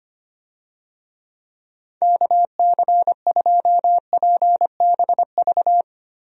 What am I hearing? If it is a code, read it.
Morse code, heard as a keyed tone, about 25 wpm, 700 Hz. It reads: KC2PBV
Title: USA Call Signs - 6 Characters - Repeated 3x - 25wpm